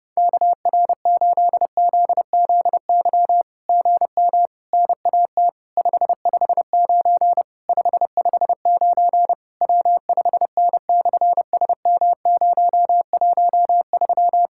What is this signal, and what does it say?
Morse code, heard as a keyed tone, about 30 wpm, 690 Hz. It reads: KR8ZZY GM NAT 559 559 W5N/SM013
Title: SOTA QSO Practice 30wpm